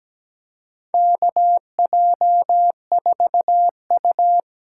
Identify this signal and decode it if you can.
Morse code, heard as a keyed tone, about 17 wpm, 700 Hz. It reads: KJ4U
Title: USA Call Signs - 6 Characters - 8x Character Spacing - 17wpm